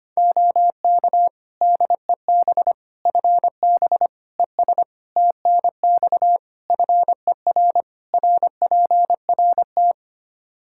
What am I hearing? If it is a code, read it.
Morse code, heard as a keyed tone, about 25 wpm, 700 Hz. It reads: OK DEB FB ES TNX FER RPRT
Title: Practice QSO - Exchange 2 - Info - 25wpm